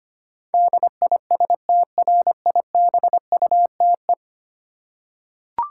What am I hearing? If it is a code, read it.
Morse code, heard as a keyed tone, about 25 wpm, 700 Hz. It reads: DISTRIBUTE
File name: 10 Letter Words - First 3 Letters Given - 25wpm